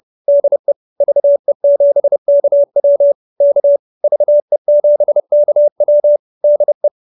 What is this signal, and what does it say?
Morse code, heard as a keyed tone, about 30 wpm, 590 Hz. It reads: DE VE7KW K VE7KW DE